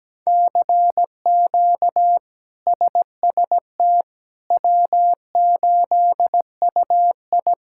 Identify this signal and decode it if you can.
Morse code, heard as a keyed tone, about 17 wpm, 700 Hz. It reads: CQ SST W8UI
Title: K1USN SST QSO Practice - 17wpm